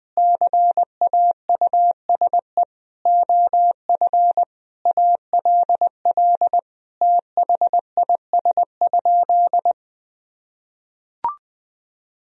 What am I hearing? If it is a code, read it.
Morse code, heard as a keyed tone, about 20 wpm, 700 Hz. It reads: CAUSE OF ALL THIS?
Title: Sentences from Top 1000 Words - Review Part 1 - 20wpm